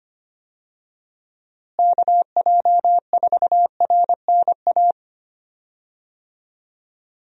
Morse code, heard as a keyed tone, about 25 wpm, 700 Hz. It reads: KJ4RNA
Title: USA Call Signs - 6 Characters - Repeated 2x - 25wpm